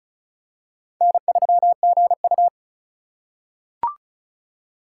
Morse code, heard as a keyed tone, about 34 wpm, 700 Hz. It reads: N3GU
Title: USA Call Signs - 4 Characters - 3x Character Spacing - 35wpm